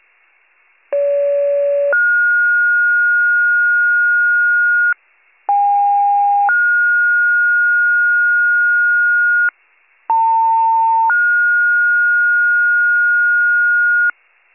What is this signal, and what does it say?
Two-tone page, heard as a one-second tone, then a three-second tone, about 570 then 1430 Hz, about 810 then 1430 Hz, about 900 then 1430 Hz. It is heard from 0.9 to 14.1 s.